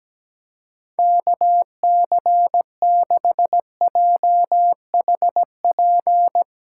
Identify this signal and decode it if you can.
Morse code, heard as a keyed tone, about 17 wpm, 700 Hz. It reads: KC6JHP
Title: USA Callsign DE Callsign 17wpm